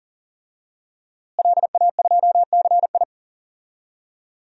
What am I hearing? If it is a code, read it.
Morse code, heard as a keyed tone, about 40 wpm, 700 Hz. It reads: LA2CI
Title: DXCC-100 Call Signs - 5 Characters - Repeated 2x - 40wpm